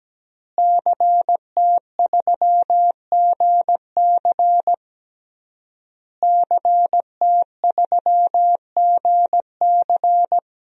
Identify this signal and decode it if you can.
Morse code, heard as a keyed tone, about 17 wpm, 700 Hz. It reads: CT3GC CT3GC
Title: DXCC-100 Call Signs - 5 Characters - Repeated 3x - 17wpm